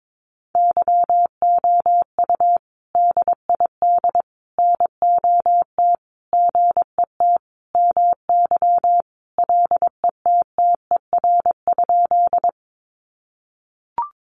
Morse code, heard as a keyed tone, about 22 wpm, 700 Hz. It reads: YOU DID NOT GET MY LETTER?
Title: Sentences from Top 600 Words 22wpm